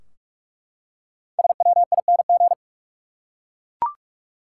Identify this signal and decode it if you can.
Morse code, heard as a keyed tone, about 43 wpm, 700 Hz. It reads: SWING
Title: Five Letter Words - 45wpm